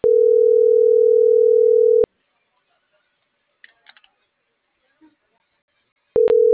Ringback tone (North America), 440 + 480 Hz.